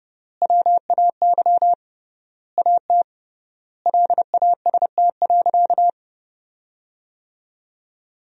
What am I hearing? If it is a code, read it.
Morse code, heard as a keyed tone, about 32 wpm, 700 Hz. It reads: WAY AT LAST.
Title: 3x Word-Spacing - Sentences from Top 300 Words - 30wpm